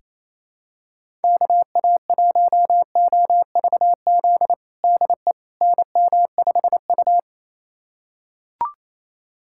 Morse code, heard as a keyed tone, about 28 wpm, 700 Hz. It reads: KA1OVZ DE NM5U
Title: USA Callsign DE Callsign 28wpm